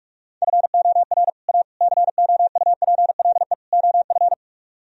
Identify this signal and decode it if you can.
Morse code, heard as a keyed tone, about 49 wpm, 700 Hz. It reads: FOR A COUPLE OF